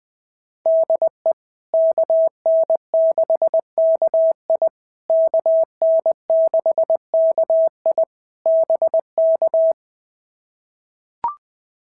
Morse code, heard as a keyed tone, about 18 wpm, 650 Hz. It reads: DE KN6KI KN6KI BK
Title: POTA QSO Practice - 20wpm